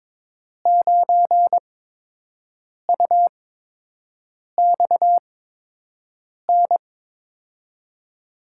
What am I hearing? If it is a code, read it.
Morse code, heard as a keyed tone, about 22 wpm, spaced as for 6 wpm, 700 Hz. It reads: 9UXN
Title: USA Call Signs - 8x Character Spacing - 22wpm